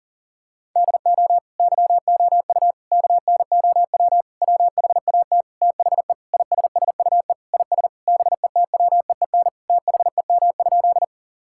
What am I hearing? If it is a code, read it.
Morse code, heard as a keyed tone, about 40 wpm, 700 Hz. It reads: DO YOU KNOW WHAT THE ISSUE IS BETWEEN THEM?
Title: Sentences from Top 1000 Words - Review Part 6 - 40wpm